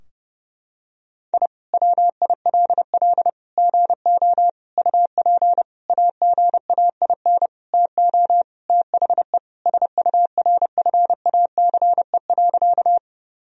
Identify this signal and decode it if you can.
Morse code, heard as a keyed tone, about 30 wpm, 700 Hz. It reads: I WILL GO UP AGAIN TO THE SURFACE.
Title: Sentences from Top 1000 Words - Review Part 3 - 30wpm